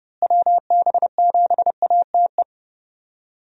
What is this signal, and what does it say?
Morse code, heard as a keyed tone, about 30 wpm, 700 Hz. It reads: WB7ATE
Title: USA Call Signs - 6 Characters - Repeated 2x - 30wpm